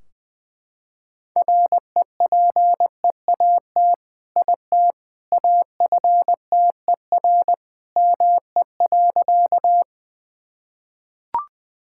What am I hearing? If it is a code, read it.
Morse code, heard as a keyed tone, about 20 wpm, 700 Hz. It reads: REPEAT IT AFTER ME.
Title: Sentences from Top 1600 Words 20wpm